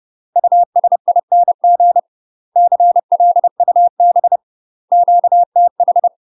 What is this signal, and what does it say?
Morse code, heard as a keyed tone, about 30 wpm, 700 Hz. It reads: USING CLUB QTH
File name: UK MASTER QSOs TRAINING / 30WPM / UK MASTER QSOs 0038